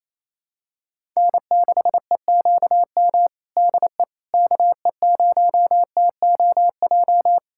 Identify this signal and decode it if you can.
Morse code, heard as a keyed tone, about 27 wpm, 700 Hz. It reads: N6EQM DE KE0TOJ
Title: USA Callsign DE Callsign 28wpm